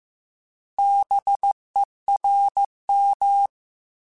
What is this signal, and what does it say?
Morse code, heard as a keyed tone, about 15 wpm, 800 Hz. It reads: BERM
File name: Alphabetical / Alphabetical mp3s / Bs @13